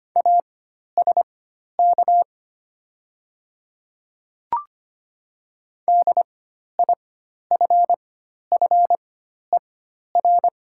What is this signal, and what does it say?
Morse code, heard as a keyed tone, about 25 wpm, spaced as for 12 wpm, 700 Hz. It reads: ASK DIFFER
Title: 4x Character Spacing - Top 200 Words - 25wpm